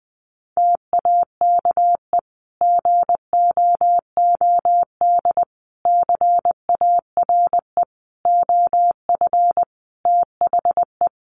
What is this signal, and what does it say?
Morse code, heard as a keyed tone, about 20 wpm, 700 Hz. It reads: TAKE GOOD CARE OF THE
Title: Sentences from Top 400 Words 20wpm